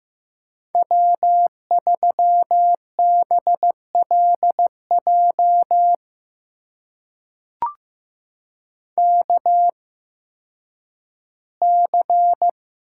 Morse code, heard as a keyed tone, about 15 wpm, spaced as for 10 wpm, 700 Hz. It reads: W3BLJ KC
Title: USA Call Signs - 5 Characters - 8x Character Spacing - 15wpm